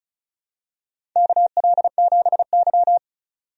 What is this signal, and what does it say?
Morse code, heard as a keyed tone, about 36 wpm, 700 Hz. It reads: KL7Y
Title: DXCC-100 Call Signs - 4 Characters - 35wpm